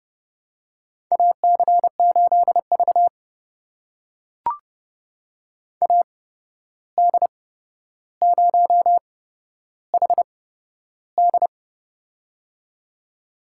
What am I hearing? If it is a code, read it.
Morse code, heard as a keyed tone, about 30 wpm, spaced as for 12 wpm, 700 Hz. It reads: AC8V AD0HD
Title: USA Call Signs - 8x Character Spacing - 30wpm